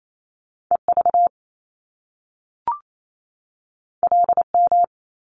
Morse code, heard as a keyed tone, about 28 wpm, 700 Hz. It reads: EV LM
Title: Top Two Letter Combinations in Words 28wpm